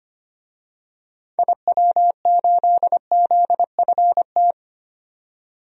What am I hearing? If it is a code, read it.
Morse code, heard as a keyed tone, about 25 wpm, 700 Hz. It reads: IW8ZFT
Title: DXCC-100 Call Signs - 6 Characters - 25wpm